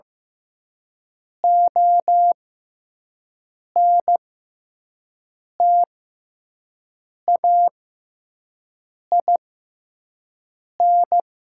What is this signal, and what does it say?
Morse code, heard as a keyed tone, about 15 wpm, spaced as for 5 wpm, 700 Hz. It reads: ONTAIN
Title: 6x Character Spacing - Top 500 Words - 15wpm